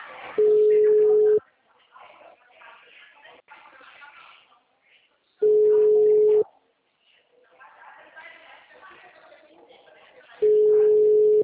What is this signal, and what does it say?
Ringback tone (Europe, 425 Hz), with speech.
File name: out- -2029- - - .84008